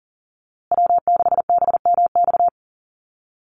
Morse code, heard as a keyed tone, about 40 wpm, 700 Hz. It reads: W6BMX
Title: USA Call Signs Repeated 3x 40wpm